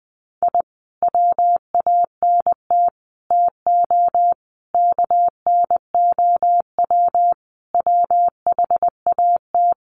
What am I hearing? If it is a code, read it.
Morse code, heard as a keyed tone, about 20 wpm, 700 Hz. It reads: I WANT TO KNOW WHAT
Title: Sets of 5 Words Encore2 20wpm